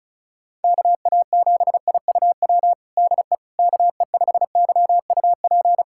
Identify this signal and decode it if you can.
Morse code, heard as a keyed tone, about 34 wpm, 700 Hz. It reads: KA7IUW DE KE5YUP
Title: USA Callsign DE Callsign - 6 Characters - 35wpm